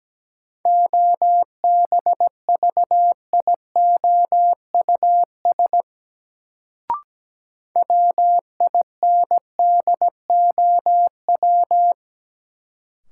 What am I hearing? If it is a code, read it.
Morse code, heard as a keyed tone, about 17 wpm, 700 Hz. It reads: OBVIOUS WINDOW
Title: Sentences from Top 1600 Words - New Word Drill Alt - 17wpm